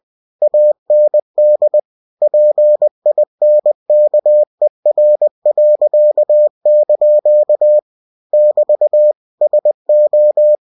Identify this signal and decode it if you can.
Morse code, heard as a keyed tone, about 20 wpm, 600 Hz. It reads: AND PINKER.) <BT> SO
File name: cw-books / winnie-the-pooh / winnie-20wpm-page-0004